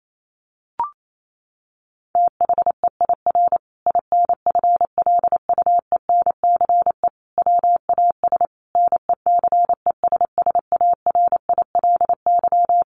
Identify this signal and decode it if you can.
Morse code, heard as a keyed tone, about 28 wpm, 700 Hz. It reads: THEIR INFLUENCE WAS NECESSARILY
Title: Sentences from Top 1400 Words 28wpm